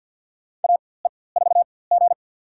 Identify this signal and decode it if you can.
Morse code, heard as a keyed tone, about 51 wpm, spaced as for 23 wpm, 700 Hz. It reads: AE4G
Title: USA Call Signs - 4x Character Spacing - 50wpm